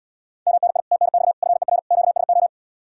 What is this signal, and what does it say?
Morse code, heard as a keyed tone, about 77 wpm, 700 Hz. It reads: MNI TNX FER QSO